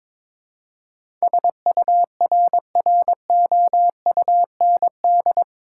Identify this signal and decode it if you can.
Morse code, heard as a keyed tone, about 22 wpm, 700 Hz. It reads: SURROUND
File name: practice / 22wpm / Eight Letter Words - 22wpm